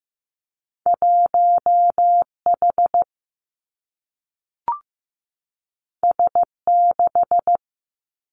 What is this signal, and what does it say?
Morse code, heard as a keyed tone, about 15 wpm, 700 Hz. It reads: 1 H S 6
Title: Two Challenging Letter-Number Combinations 15wpm